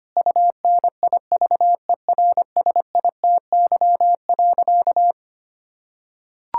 Morse code, heard as a keyed tone, about 25 wpm, 700 Hz. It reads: UNIVERSITY.
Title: Sentences from Top 1000 Words - Review Part 5 - 25wpm